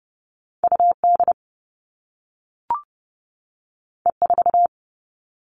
Morse code, heard as a keyed tone, about 30 wpm, 700 Hz. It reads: UD E4
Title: All Cut Numbers - 2-Digit 30wpm